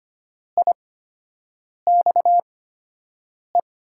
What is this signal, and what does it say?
Morse code, heard as a keyed tone, about 25 wpm, spaced as for 7 wpm, 700 Hz. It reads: IXE
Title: USA Call Signs - 6 Characters - 8x Character Spacing - 25wpm